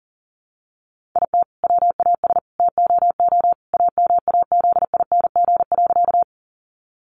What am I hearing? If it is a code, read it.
Morse code, heard as a keyed tone, about 40 wpm, 700 Hz. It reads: IT WAS TOO AMAZING.
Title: Sentences from Top 1500 Words 40wpm